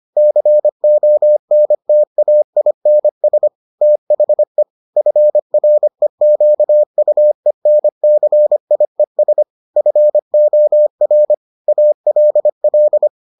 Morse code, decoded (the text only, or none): CONTAINS THE FREQUENCIES FOR ALL